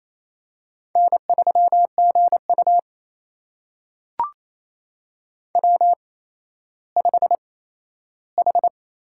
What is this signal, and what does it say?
Morse code, heard as a keyed tone, about 28 wpm, spaced as for 14 wpm, 700 Hz. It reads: N3GU W5H
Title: USA Call Signs - 4 Characters - 8x Character Spacing - 28wpm